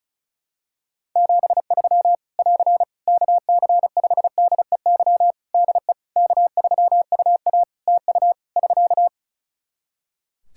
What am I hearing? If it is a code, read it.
Morse code, heard as a keyed tone, about 36 wpm, 700 Hz. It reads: 73 <AR> KC5DEY DE K3UA TU <SK>
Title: Practice QSO - Exchange 4 - Ending - 35wpm